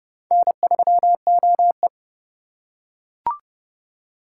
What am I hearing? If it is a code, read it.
Morse code, heard as a keyed tone, about 28 wpm, 700 Hz. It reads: N3OE